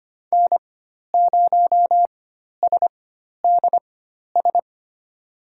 Morse code, heard as a keyed tone, about 25 wpm, spaced as for 12 wpm, 700 Hz. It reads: N0SDS
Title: USA Call Signs - 5 Characters - 4x Character Spacing - 25wpm